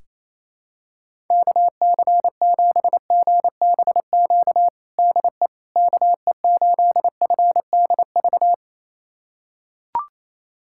Morse code, heard as a keyed tone, about 28 wpm, 700 Hz. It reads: KC7GBQ DE KE8FDV
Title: USA Callsign DE Callsign - 6 Characters - 28wpm